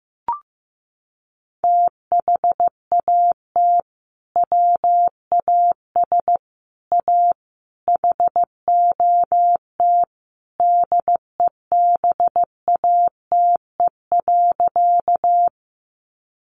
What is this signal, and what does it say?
Morse code, heard as a keyed tone, about 15 wpm, 700 Hz. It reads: THAT WAS A HOT DEBATE.
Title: Sentences from Top 1100 Words 15wpm